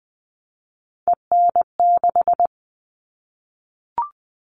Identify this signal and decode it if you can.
Morse code, heard as a keyed tone, about 20 wpm, 700 Hz. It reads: EN6
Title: All Cut Numbers - 3-Digit 20wpm